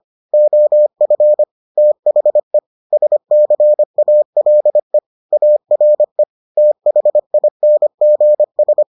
Morse code, heard as a keyed tone, about 25 wpm, 600 Hz. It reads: OF THE SCALE ARE THINGS